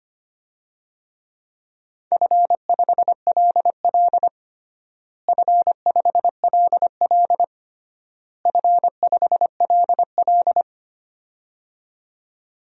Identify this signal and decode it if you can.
Morse code, heard as a keyed tone, about 25 wpm, 700 Hz. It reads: F5LL F5LL F5LL
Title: DXCC-100 Call Signs - 4 Characters - Repeated 3x - 25wpm